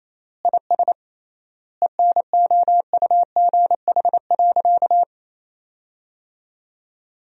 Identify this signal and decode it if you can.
Morse code, heard as a keyed tone, about 28 wpm, 700 Hz. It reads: IS ENOUGH.
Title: 3x Word-Spacing - Sentences from Top 400 Words - 28wpm